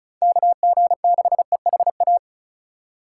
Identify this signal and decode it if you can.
Morse code, heard as a keyed tone, about 34 wpm, 700 Hz. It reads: KG6EHA
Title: USA Call Signs - 6 Characters - 3x Character Spacing - 35wpm